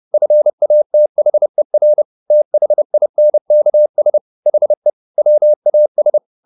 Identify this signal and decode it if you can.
Morse code, heard as a keyed tone, about 30 wpm, 600 Hz. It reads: FATHER THINKS HE WAS
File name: the old man and the sea / the old man and the sea 1 30wpm